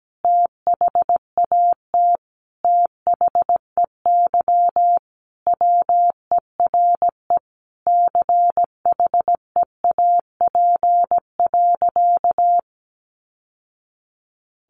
Morse code, heard as a keyed tone, about 17 wpm, 700 Hz. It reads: THAT THEY WERE CHEAP.
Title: Sentences from Top 1200 Words 17wpm